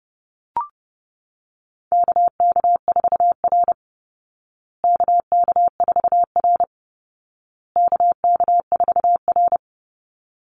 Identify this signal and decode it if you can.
Morse code, heard as a keyed tone, about 30 wpm, 700 Hz. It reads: KK4R KK4R KK4R
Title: Call Signs Repeated 2x - 30wpm